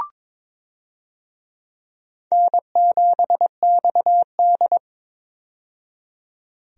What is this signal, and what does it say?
Morse code, heard as a keyed tone, about 22 wpm, 700 Hz. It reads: N7XD
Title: USA Call Signs - 4 Characters - 22wpm